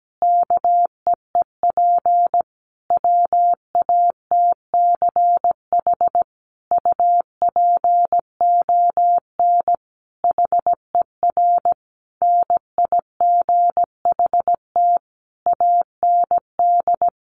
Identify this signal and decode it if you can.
Morse code, heard as a keyed tone, about 17 wpm, 700 Hz. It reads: KEEP WATCH UPON HER NIGHT AND
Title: Sentences from Top 1100 Words 17wpm